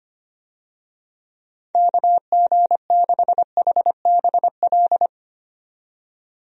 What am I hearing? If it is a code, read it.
Morse code, heard as a keyed tone, about 25 wpm, 700 Hz. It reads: KG6HBL
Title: USA Call Signs - 6 Characters - 25wpm